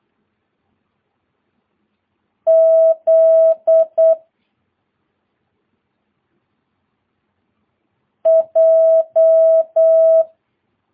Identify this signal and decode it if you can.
Morse code, heard as a keyed tone, about 8 wpm, 660 Hz. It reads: Z J
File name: jh8cbh / 2022cw